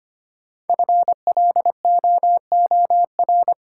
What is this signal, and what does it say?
Morse code, heard as a keyed tone, about 25 wpm, 700 Hz. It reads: FLOOR
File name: 5 Letter Words - First 3 Letters Given - 25wpm